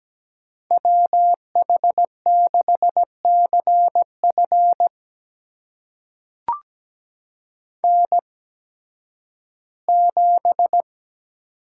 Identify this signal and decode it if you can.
Morse code, heard as a keyed tone, about 17 wpm, spaced as for 11 wpm, 700 Hz. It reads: WH6CF N7
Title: USA Call Signs - 8x Character Spacing - 17wpm